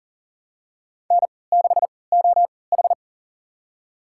Morse code, heard as a keyed tone, about 39 wpm, spaced as for 23 wpm, 700 Hz. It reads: N6OH
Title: USA Call Signs - 4 Characters - 3x Character Spacing - 40wpm